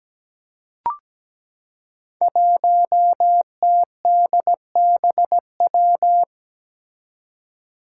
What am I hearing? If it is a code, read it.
Morse code, heard as a keyed tone, about 17 wpm, 700 Hz. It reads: 1TDBW
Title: Five Challenging Letter-Number Combinations 17wpm